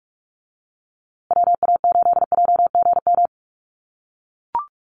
Morse code, heard as a keyed tone, about 44 wpm, 700 Hz. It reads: WA8JGM
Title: USA Call Signs Repeated 2x 45wpm